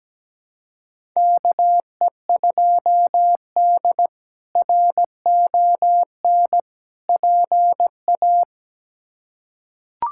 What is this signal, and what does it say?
Morse code, heard as a keyed tone, about 17 wpm, 700 Hz. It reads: KE2D RON PA
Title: K1USN SST QSO Practice - 17wpm